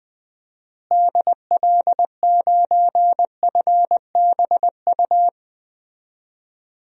Morse code, heard as a keyed tone, about 20 wpm, 700 Hz. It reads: DL9FBU
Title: DXCC-100 Call Signs - 6 Characters - Repeated 3x - 20wpm